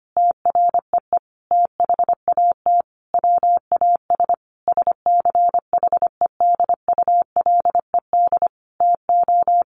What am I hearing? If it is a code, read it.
Morse code, heard as a keyed tone, about 25 wpm, 700 Hz. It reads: TREE THAT WAS SCHEDULED TO
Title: A Sound Of Thunder 25wpm